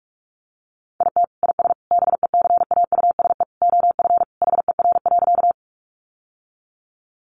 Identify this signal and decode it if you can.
Morse code, heard as a keyed tone, about 44 wpm, 700 Hz. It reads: IT IS BECAUSE OF HER.
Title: Sentences from Top 600 Words 45wpm